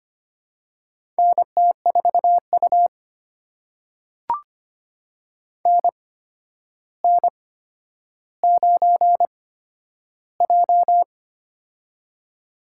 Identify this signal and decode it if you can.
Morse code, heard as a keyed tone, about 25 wpm, spaced as for 11 wpm, 700 Hz. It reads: NT4U NN9J
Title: USA Call Signs - 4 Characters - 8x Character Spacing - 25wpm